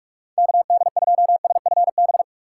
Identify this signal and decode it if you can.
Morse code, heard as a keyed tone, about 41 wpm, 700 Hz. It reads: KD2SFB
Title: USA Call Signs - 6 Characters - 2x Character Spacing - 45wpm